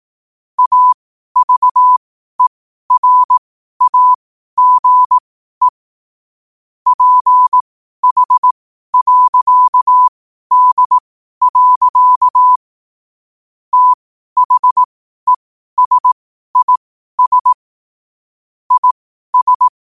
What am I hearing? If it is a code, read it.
Morse code, heard as a keyed tone, about 18 wpm, spaced as for 12 wpm, 1000 Hz. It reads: AVERAGE PH.D. THESIS IS
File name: qotdfiles / 12wpm